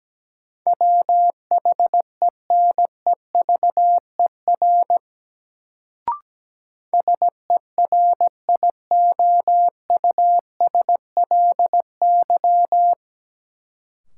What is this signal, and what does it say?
Morse code, heard as a keyed tone, about 17 wpm, 700 Hz. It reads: WHENEVER SERIOUSLY
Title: Sentences from Top 1200 Words - New Word Drill Alt - 17wpm